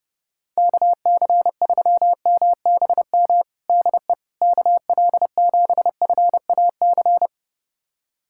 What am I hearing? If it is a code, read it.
Morse code, heard as a keyed tone, about 30 wpm, 700 Hz. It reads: KC3MBM DE KL7FAC
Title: USA Callsign DE Callsign 30wpm